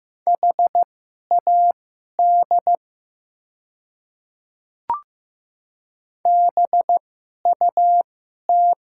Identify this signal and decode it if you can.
Morse code, heard as a keyed tone, about 15 wpm, spaced as for 11 wpm, 700 Hz. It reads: HAD BUT
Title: 2x Character Spacing - Top 100 Words - 15wpm